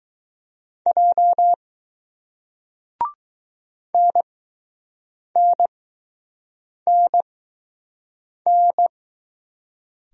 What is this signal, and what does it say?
Morse code, heard as a keyed tone, about 20 wpm, spaced as for 6 wpm, 700 Hz. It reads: J NNNN